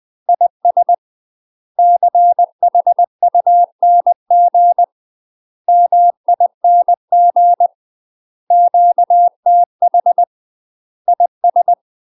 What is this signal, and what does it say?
Morse code, heard as a keyed tone, about 20 wpm, 700 Hz. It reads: IS CHUNG MING QTH IS
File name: UK MASTER QSOs TRAINING / 20WPM / UK MASTER QSOs 0067